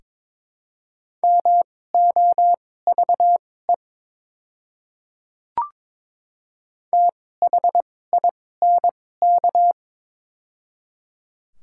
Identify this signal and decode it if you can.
Morse code, heard as a keyed tone, about 22 wpm, spaced as for 16 wpm, 700 Hz. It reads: MOVE THINK
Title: 2x Character Spacing - Top 200 Words - 22wpm